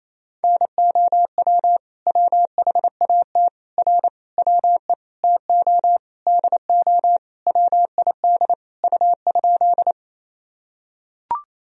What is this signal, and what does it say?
Morse code, heard as a keyed tone, about 28 wpm, 700 Hz. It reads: NOW WHAT R WE TO DO WID U?
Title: Sentences from Top 100 Words and Abbreviations 28wpm